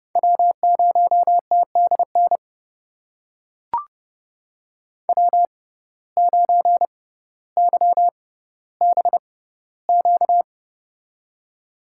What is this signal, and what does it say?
Morse code, heard as a keyed tone, about 30 wpm, spaced as for 15 wpm, 700 Hz. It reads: W0TDN W9YBQ
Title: USA Call Signs - 6x Character Spacing - 30wpm